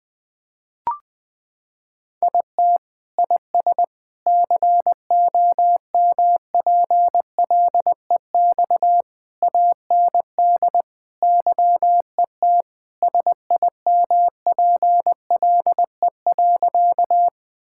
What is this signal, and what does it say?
Morse code, heard as a keyed tone, about 20 wpm, 700 Hz. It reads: IT IS COMPLEX AND YET SIMPLE.
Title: Sentences from Top 1000 Words - Review Part 6 - 20wpm